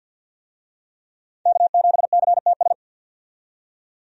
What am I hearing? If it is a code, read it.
Morse code, heard as a keyed tone, about 52 wpm, 700 Hz. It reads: K7CTS
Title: USA Call Signs - 5 Characters - 50wpm